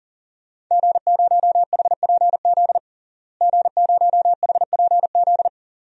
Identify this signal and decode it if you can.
Morse code, heard as a keyed tone, about 40 wpm, 700 Hz. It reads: G0HPZ G0HPZ
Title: DXCC-100 Call Signs - 5 Characters - Repeated 2x - 40wpm